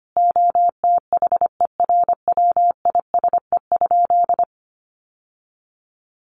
Morse code, heard as a keyed tone, about 25 wpm, 700 Hz. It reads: OTHERWISE?
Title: Sentences from Top 1200 Words 25wpm